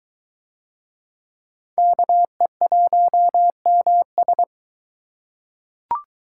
Morse code, heard as a keyed tone, about 21 wpm, 700 Hz. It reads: KE1MS